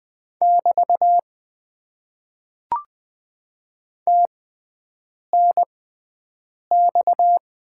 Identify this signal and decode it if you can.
Morse code, heard as a keyed tone, about 20 wpm, 700 Hz. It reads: <BT> T N X
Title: Top 90 QSO Elements - 6x Character Spacing - 20wpm